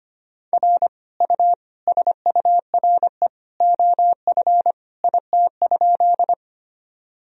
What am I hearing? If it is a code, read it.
Morse code, heard as a keyed tone, about 25 wpm, 700 Hz. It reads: R U SURE OF IT?
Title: Sentences from Top 300 Words and Abbreviations 25wpm